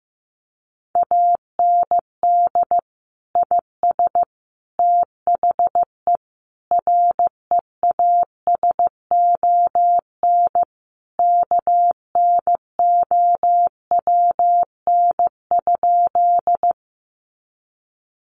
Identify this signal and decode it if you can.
Morse code, heard as a keyed tone, about 15 wpm, 700 Hz. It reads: AND IS THE REASON KNOWN?
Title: Sentences from Top 700 Words 15wpm